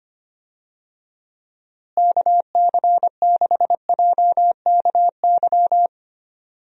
Morse code, heard as a keyed tone, about 25 wpm, 700 Hz. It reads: KC6JKY
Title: USA Call Signs - 6 Characters - Repeated 2x - 25wpm